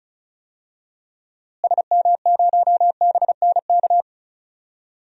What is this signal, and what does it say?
Morse code, heard as a keyed tone, about 34 wpm, 700 Hz. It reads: SM0BNK
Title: DXCC-100 Call Signs - 6 Characters - Repeated 3x - 35wpm